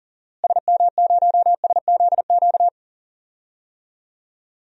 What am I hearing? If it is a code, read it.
Morse code, heard as a keyed tone, about 39 wpm, 700 Hz. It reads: SM0SZQ
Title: DXCC-100 Call Signs - 6 Characters - Repeated 3x - 40wpm